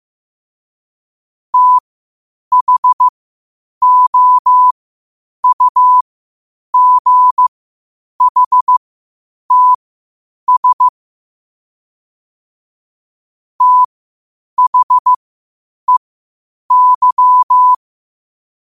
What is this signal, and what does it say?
Morse code, heard as a keyed tone, about 15 wpm, spaced as for 8 wpm, 1000 Hz. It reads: THOUGHTS THEY
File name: qotdfiles / 8wpm